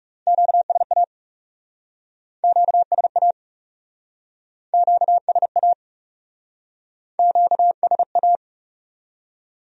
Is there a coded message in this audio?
Yes: Morse code from 0.3 to 8.4 s, heard as a keyed tone, about 40 wpm, 700 Hz. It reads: QSA QSA QSA QSA